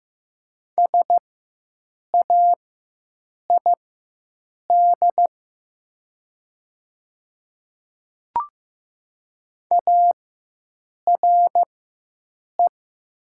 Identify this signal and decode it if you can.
Morse code, heard as a keyed tone, about 15 wpm, spaced as for 7 wpm, 700 Hz. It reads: SAID ARE
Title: 4x Character Spacing - Top 200 Words - 15wpm